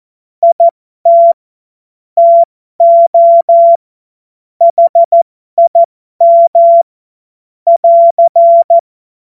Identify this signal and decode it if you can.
Morse code, heard as a keyed tone, about 14 wpm, spaced as for 12 wpm, 680 Hz. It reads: IT TO HIM <AR>
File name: Morsekurs / Witze